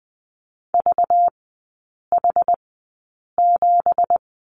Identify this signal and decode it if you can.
Morse code, heard as a keyed tone, about 20 wpm, spaced as for 8 wpm, 700 Hz. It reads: VH7
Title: 191 Lesson 31e - Letter v - Three Characters 20wpm